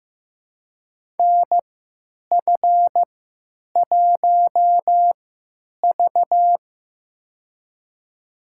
Morse code, heard as a keyed tone, about 15 wpm, spaced as for 9 wpm, 700 Hz. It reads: NF1V
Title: USA Call Signs - 4 Characters - 3x Character Spacing - 15wpm